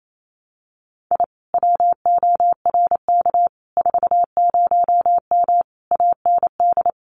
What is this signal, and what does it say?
Morse code, heard as a keyed tone, about 28 wpm, 700 Hz. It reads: I WORK 40M AND
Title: Common QSO Phrases Encore 28wpm